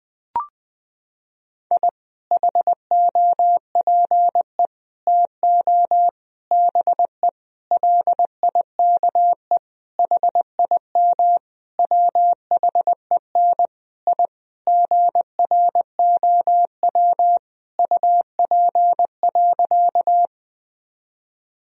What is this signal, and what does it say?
Morse code, heard as a keyed tone, about 20 wpm, 700 Hz. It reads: I HOPE TO BE LIKE HIM WHEN I GROW UP.
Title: Sentences from Top 1000 Words - Review Part 1 - 20wpm